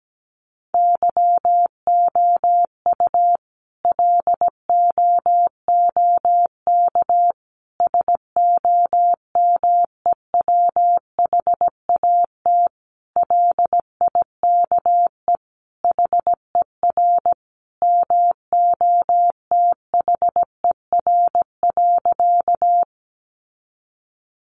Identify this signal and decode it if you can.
Morse code, heard as a keyed tone, about 17 wpm, 700 Hz. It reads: YOU LOOK SOMEWHAT LIKE HER MOTHER.
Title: Sentences from Top 1300 Words 17wpm